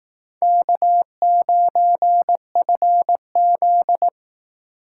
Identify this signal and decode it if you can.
Morse code, heard as a keyed tone, about 19 wpm, 700 Hz. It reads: K9FZ